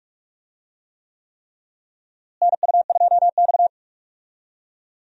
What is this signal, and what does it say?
Morse code, heard as a keyed tone, about 45 wpm, 700 Hz. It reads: NU2X
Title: USA Call Signs - 4 Characters - Repeated 3x - 45wpm